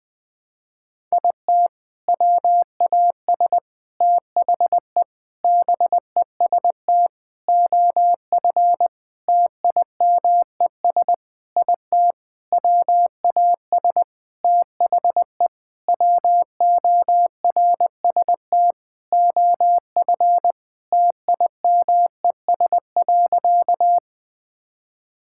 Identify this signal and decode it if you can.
Morse code, heard as a keyed tone, about 20 wpm, 700 Hz. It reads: IT WAS THE BEST OF TIMES IT WAS THE WORST OF TIMES.
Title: Sentences with 8-Word Phrases - 20wpm